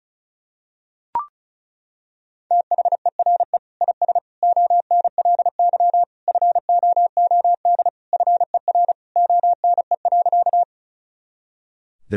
Morse code, heard as a keyed tone, about 36 wpm, 700 Hz. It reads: THERE IS ONLY FOOD FER ONE.
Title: Sentences from Top 300 Words and Abbreviations 35wpm